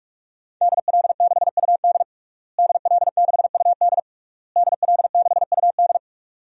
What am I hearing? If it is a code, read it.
Morse code, heard as a keyed tone, about 46 wpm, 700 Hz. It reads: DL6UD DL6UD DL6UD
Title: DXCC-100 Call Signs - 5 Characters - Repeated 3x - 45wpm